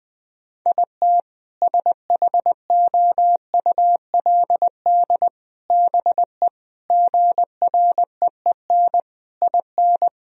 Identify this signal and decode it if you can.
Morse code, heard as a keyed tone, about 21 wpm, 700 Hz. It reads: IT SHOULD BE GREEN IN
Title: Sentences from Top 1000 Words - Review Part 2 - 20wpm